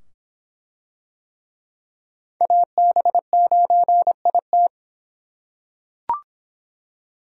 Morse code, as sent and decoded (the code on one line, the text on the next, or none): .- -... ----. .. -
AB9IT